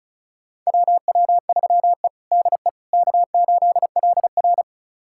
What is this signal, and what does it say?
Morse code, heard as a keyed tone, about 36 wpm, 700 Hz. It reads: WW3E DE K8LR
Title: USA Callsign DE Callsign - 4 Characters - 35wpm